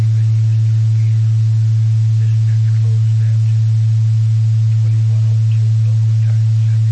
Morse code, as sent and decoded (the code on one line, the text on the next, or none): none